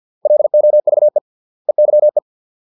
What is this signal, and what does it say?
Morse code, heard as a keyed tone, about 50 wpm, 600 Hz. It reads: LOVE EYE